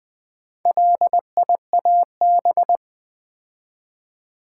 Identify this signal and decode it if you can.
Morse code, heard as a keyed tone, about 21 wpm, 700 Hz. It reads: LIAB
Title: Build A Word - 2 Letters At A Time - 6-Letter Words - 20wpm